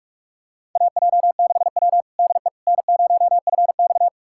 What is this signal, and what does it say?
Morse code, heard as a keyed tone, about 46 wpm, 700 Hz. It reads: AJ6W DE N0FX
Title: USA Callsign DE Callsign - 4 Characters - 45wpm